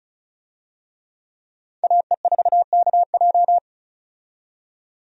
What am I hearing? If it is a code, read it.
Morse code, heard as a keyed tone, about 36 wpm, 700 Hz. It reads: AE4KJ
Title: USA Call Signs - 5 Characters - 35wpm